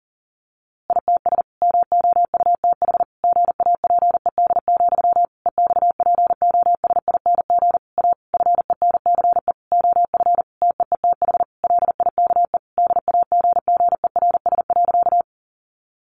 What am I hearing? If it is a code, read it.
Morse code, heard as a keyed tone, about 40 wpm, 700 Hz. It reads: ITS MOUTH GAPED, EXPOSING A FENCE OF TEETH LIKE DAGGERS.
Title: A Sound Of Thunder 40wpm